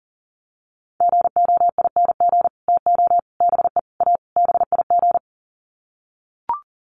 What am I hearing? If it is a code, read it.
Morse code, heard as a keyed tone, about 40 wpm, 700 Hz. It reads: GOING TO BE A BIG
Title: Sets of 5 Words Encore2 40wpm